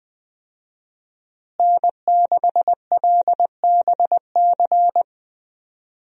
Morse code, heard as a keyed tone, about 20 wpm, 700 Hz. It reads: N6LBC
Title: USA Call Signs - 5 Characters - Repeated 3x - 20wpm